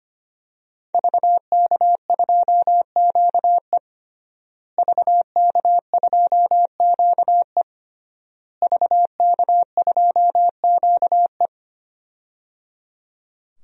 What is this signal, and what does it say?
Morse code, heard as a keyed tone, about 25 wpm, 700 Hz. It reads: VK2QE VK2QE VK2QE
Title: DXCC-100 Call Signs - 5 Characters - Repeated 3x - 25wpm